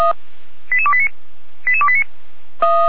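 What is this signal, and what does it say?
Five-tone selective call (ZVEI), heard as quick runs of five tones, tones of about 70 ms. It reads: 88289 88289
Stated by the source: ZVEI five-tone selective call received by SDR.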